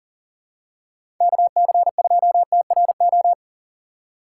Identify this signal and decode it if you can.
Morse code, heard as a keyed tone, about 40 wpm, 700 Hz. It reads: KC2TRO
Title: USA Call Signs - 6 Characters - 2x Character Spacing - 40wpm